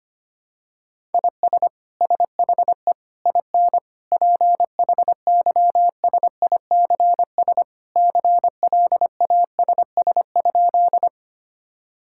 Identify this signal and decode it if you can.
Morse code, heard as a keyed tone, about 25 wpm, 700 Hz. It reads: IS SHE IN PHYSICS CLASS?
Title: Sentences from Top 1000 Words - Review Part 5 - 25wpm